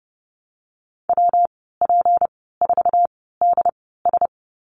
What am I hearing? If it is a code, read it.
Morse code, heard as a keyed tone, about 30 wpm, spaced as for 17 wpm, 700 Hz. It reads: WP4DS
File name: Lesson 37g - Letter x - Call Signs 30wpm